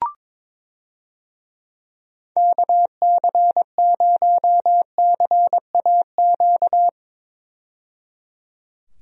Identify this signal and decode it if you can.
Morse code, heard as a keyed tone, about 22 wpm, 700 Hz. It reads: KC0CAQ
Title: USA Call Signs - 6 Characters - 22wpm